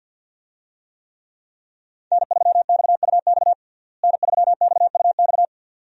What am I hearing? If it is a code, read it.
Morse code, heard as a keyed tone, about 47 wpm, 700 Hz. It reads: N3XUX N3XUX
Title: USA Call Signs - 5 Characters - Repeated 2x - 50wpm